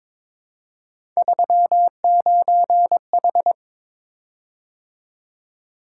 Morse code, heard as a keyed tone, about 22 wpm, 700 Hz. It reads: 39H
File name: Three Letter-Number Combinations - 22wpm